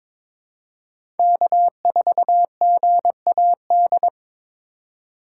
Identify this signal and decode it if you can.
Morse code, heard as a keyed tone, about 22 wpm, 700 Hz. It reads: K4GAD
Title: USA Call Signs - 5 Characters - 2x Character Spacing - 22wpm